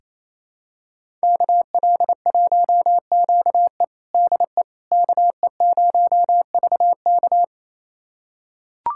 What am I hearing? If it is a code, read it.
Morse code, heard as a keyed tone, about 28 wpm, 700 Hz. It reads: KL1QE DE KE0VK
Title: USA Callsign DE Callsign - 5 Characters - 28wpm